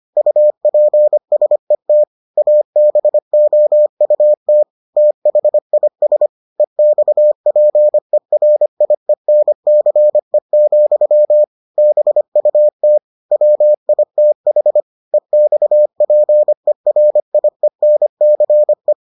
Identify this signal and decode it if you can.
Morse code, heard as a keyed tone, about 25 wpm, 600 Hz. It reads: UPSET ABOUT THIS EXPERIENCE, BUT WITH EXPERIENCE